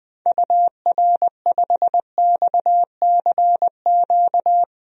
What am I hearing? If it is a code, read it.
Morse code, heard as a keyed tone, about 21 wpm, 700 Hz. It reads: UR5XCQ
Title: DXCC-100 Call Signs - 6 Characters - 20wpm